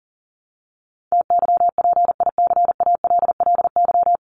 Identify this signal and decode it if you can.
Morse code, heard as a keyed tone, about 37 wpm, 700 Hz. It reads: TYPICALLY
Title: LY Suffix - Top 100 Words with Base Words 40wpm